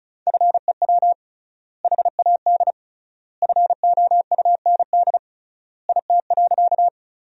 Morse code, heard as a keyed tone, about 36 wpm, 700 Hz. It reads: FEW HAD FOUND IT.
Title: 3x Word-Spacing - Sentences from Top 300 Words - 35wpm